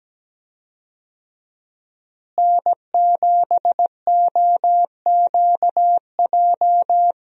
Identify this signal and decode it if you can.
Morse code, heard as a keyed tone, about 17 wpm, 700 Hz. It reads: N7OQJ
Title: USA Call Signs - 5 Characters - Repeated 3x - 17wpm